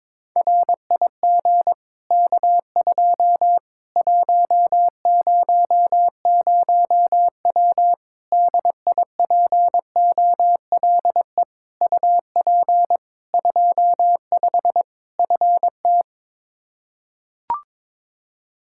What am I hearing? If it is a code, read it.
Morse code, heard as a keyed tone, about 21 wpm, 700 Hz. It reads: RIG K2 100W DIPOLE UP 25 FT
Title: Practice QSO - Exchange 2 - Info - 22wpm